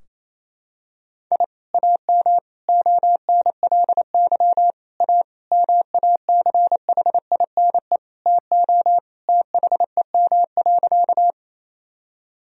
Morse code, heard as a keyed tone, about 28 wpm, 700 Hz. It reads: I AM ONLY A MACHINE TO THEM.
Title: Sentences from Top 1000 Words - Review Part 2 - 28wpm